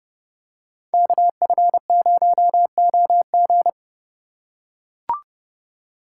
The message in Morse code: KF0OG

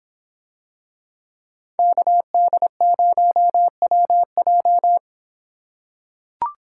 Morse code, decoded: KD0WJ

